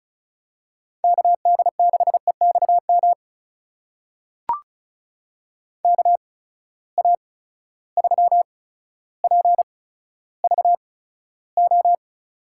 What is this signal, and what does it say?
Morse code, heard as a keyed tone, about 36 wpm, spaced as for 15 wpm, 700 Hz. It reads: KD6EXM KA3PVO
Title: USA Call Signs - 6 Characters - 8x Character Spacing - 35wpm